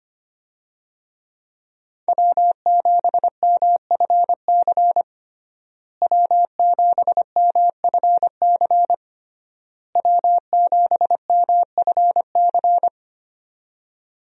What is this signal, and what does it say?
Morse code, heard as a keyed tone, about 25 wpm, 700 Hz. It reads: W7MFC W7MFC W7MFC
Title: USA Call Signs - 5 Characters - Repeated 3x - 25wpm